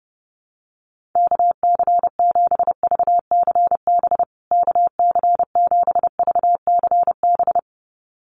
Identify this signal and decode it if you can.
Morse code, heard as a keyed tone, about 30 wpm, 700 Hz. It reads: KC7VCB KC7VCB
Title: USA Call Signs Repeated 2x 30wpm